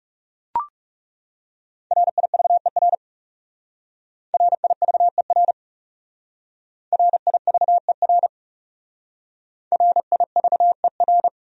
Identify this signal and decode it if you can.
Morse code, heard as a keyed tone, about 36 wpm, 700 Hz. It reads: RIVER RIVER RIVER RIVER
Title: Speed-Racing Top 500 Words 30wpm